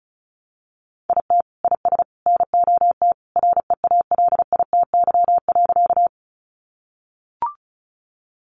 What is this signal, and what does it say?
Morse code, heard as a keyed tone, about 34 wpm, 700 Hz. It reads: IT IS NOT REALITY.
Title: Sentences from Top 1200 Words 35wpm